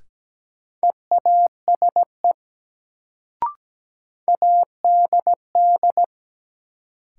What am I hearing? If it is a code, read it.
Morse code, heard as a keyed tone, about 17 wpm, 700 Hz. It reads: EASE ADD
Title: Sentences from Top 1000 Words - Review Part 1 - New Word Drill - 17wpm